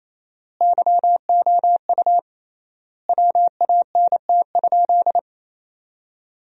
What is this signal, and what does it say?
Morse code, heard as a keyed tone, about 28 wpm, 700 Hz. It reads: YOU WANT?
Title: 3x Word-Spacing - Sentences from Top 200 Words - 28wpm